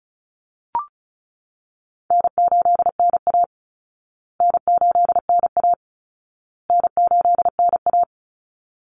Morse code, heard as a keyed tone, about 34 wpm, 700 Hz. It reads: N8NA N8NA N8NA
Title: Call Signs Repeated 2x - 35wpm